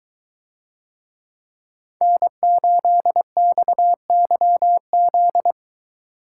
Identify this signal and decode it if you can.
Morse code, heard as a keyed tone, about 23 wpm, 700 Hz. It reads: N8XYZ